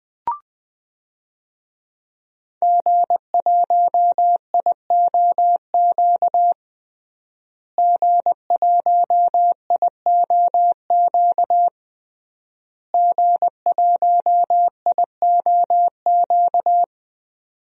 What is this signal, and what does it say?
Morse code, heard as a keyed tone, about 20 wpm, 700 Hz. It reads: G1IOQ G1IOQ G1IOQ
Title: DXCC-100 Call Signs - 5 Characters - Repeated 3x - 20wpm